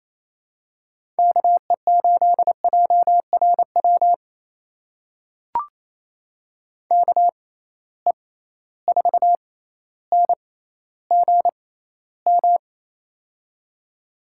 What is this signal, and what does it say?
Morse code, heard as a keyed tone, about 28 wpm, spaced as for 14 wpm, 700 Hz. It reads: KE8JRW KE4NGM
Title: USA Call Signs - 6 Characters - 6x Character Spacing - 28wpm